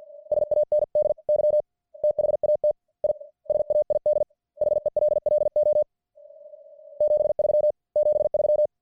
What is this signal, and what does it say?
Morse code, heard as a keyed tone, about 50 wpm, 600 Hz. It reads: SANDY THAT I SAID HELLO 73 73